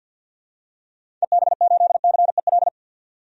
Morse code, heard as a keyed tone, about 53 wpm, 700 Hz. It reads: EB8CEL
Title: DXCC-100 Call Signs - 6 Characters - 50wpm